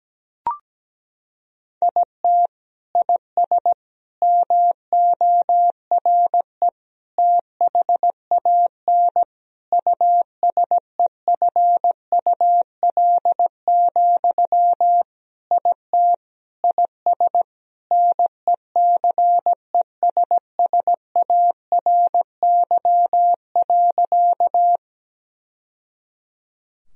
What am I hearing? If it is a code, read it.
Morse code, heard as a keyed tone, about 17 wpm, 700 Hz. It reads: IT IS MORE THAN USEFUL, IT IS NECESSARY.
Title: Sentences from Top 1000 Words - Review Part 4 - 17wpm